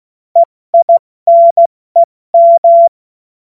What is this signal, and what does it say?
Morse code, heard as a keyed tone, about 16 wpm, spaced as for 14 wpm, 680 Hz. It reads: EINEM
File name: Witz-30-September-2025-70-F80